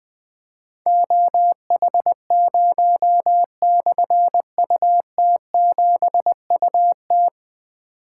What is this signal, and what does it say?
Morse code, heard as a keyed tone, about 20 wpm, 700 Hz. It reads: OH0/UT7UT
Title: Difficult Call Signs 20wpm